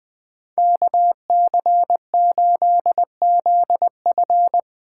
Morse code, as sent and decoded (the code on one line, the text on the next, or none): -.- -.-. ---.. --.. ..-.
KC8ZF